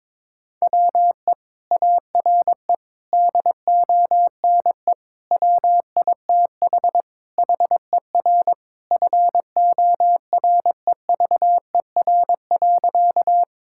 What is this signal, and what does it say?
Morse code, heard as a keyed tone, about 21 wpm, 700 Hz. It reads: WE ARE DONE WITH HER FOREVER.
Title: Sentences from Top 1600 Words 22wpm